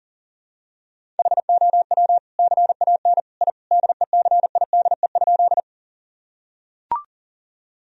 Morse code, heard as a keyed tone, about 40 wpm, 700 Hz. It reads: HOW CAN I DECIDE?
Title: Sentences from Top 1000 Words - Review Part 3 - 40wpm